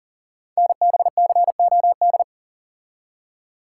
Morse code, heard as a keyed tone, about 40 wpm, 700 Hz. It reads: NBCOD
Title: Code Groups - 5 Letters - 40wpm